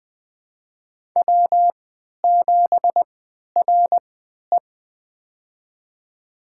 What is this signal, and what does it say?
Morse code, heard as a keyed tone, about 20 wpm, spaced as for 11 wpm, 700 Hz. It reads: W7RE
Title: USA Call Signs - 4 Characters - 3x Character Spacing - 20wpm